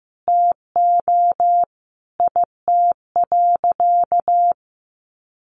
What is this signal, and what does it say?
Morse code, heard as a keyed tone, about 15 wpm, 700 Hz. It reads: TO IT.
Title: Sentences from Top 1200 Words 15wpm